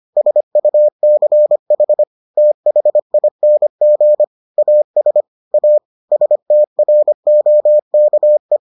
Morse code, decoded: SUCH THING AS A STROKE

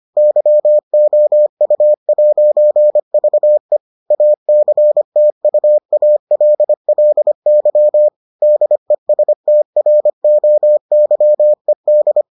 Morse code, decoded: YOU'VE ACTUALLY DESTROYED

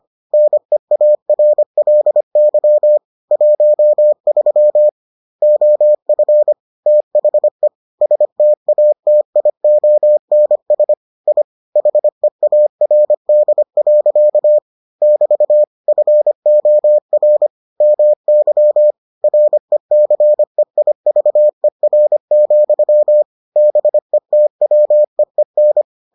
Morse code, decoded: NEARLY 13 OF THE STATIONS I HEARD. <BT> FOR MY RECEIVER, BETWEEN

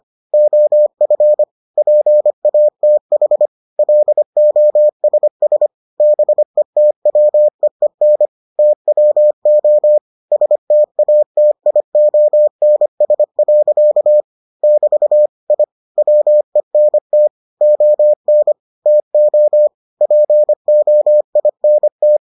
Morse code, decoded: OF PATH LOSS BETWEEN TWO STATIONS. <BT> I WENT ON TO POINT

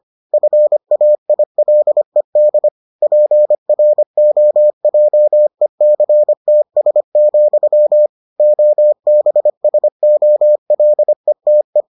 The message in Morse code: FAILED PROJECTS, OBSOLETE

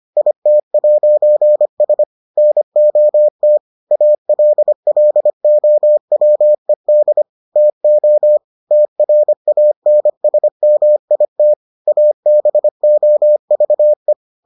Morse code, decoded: IT'S NOT ALLOWED TO TRANSMIT ABOVE

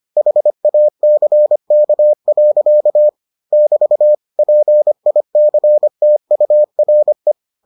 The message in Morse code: HACK. <BT> PICTURE

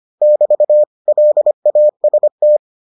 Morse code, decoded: <BT> LAST

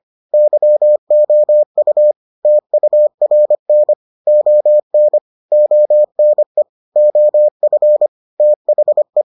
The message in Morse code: YOU TURN ON ONE OF THE

